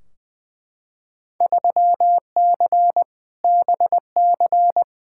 Morse code, heard as a keyed tone, about 20 wpm, 700 Hz. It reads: 3C BC
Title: Field Day Exchanges Encore - 20wpm